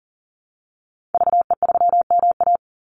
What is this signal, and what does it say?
Morse code, heard as a keyed tone, about 40 wpm, 700 Hz. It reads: VE3MA
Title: Call Signs Repeated 3x - 40wpm